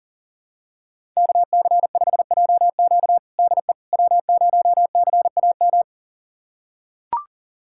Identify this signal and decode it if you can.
Morse code, heard as a keyed tone, about 40 wpm, 700 Hz. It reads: KC5JQ DE W0CAM
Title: USA Callsign DE Callsign - 5 Characters - 40wpm